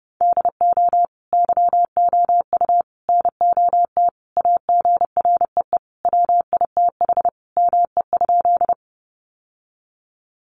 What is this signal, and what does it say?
Morse code, heard as a keyed tone, about 30 wpm, 700 Hz. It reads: DO YOU NOT AGREE WITH ME?
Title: Sentences from Top 1100 Words 30wpm